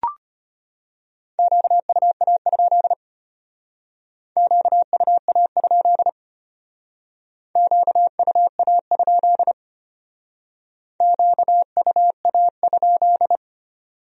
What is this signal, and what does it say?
Morse code, heard as a keyed tone, about 32 wpm, 700 Hz. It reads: QUA? QUA? QUA? QUA?